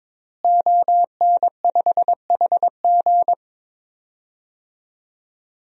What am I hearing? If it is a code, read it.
Morse code, heard as a keyed tone, about 22 wpm, 700 Hz. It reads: ON5HG
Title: DXCC-100 Call Signs - 5 Characters - Repeated 3x - 22wpm